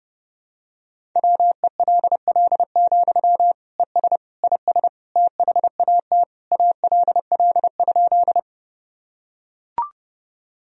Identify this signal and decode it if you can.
Morse code, heard as a keyed tone, about 30 wpm, 700 Hz. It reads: WELL, ES IS THAT ALL?
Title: Sentences from Top 200 Words and Abbreviations 30wpm